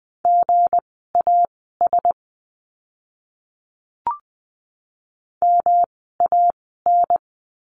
Morse code, heard as a keyed tone, about 20 wpm, spaced as for 15 wpm, 700 Hz. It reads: GAS MAN
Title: 160 Lesson 26f - Letter g - Words 20wpm